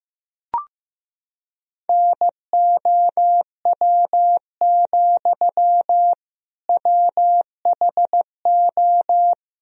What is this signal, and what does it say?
Morse code, heard as a keyed tone, about 16 wpm, 700 Hz. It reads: NOW, WHO
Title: Sentences from Top 1000 Words - Review Part 1 - 15wpm